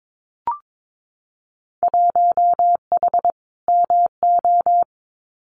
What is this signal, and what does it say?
Morse code, heard as a keyed tone, about 22 wpm, 700 Hz. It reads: 1H MO
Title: Winter Field Day Exchanges 22wpm